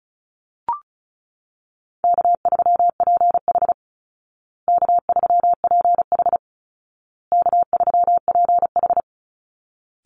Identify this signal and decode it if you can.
Morse code, heard as a keyed tone, about 34 wpm, 700 Hz. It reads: K3PH K3PH K3PH
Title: Call Signs Repeated 2x - 35wpm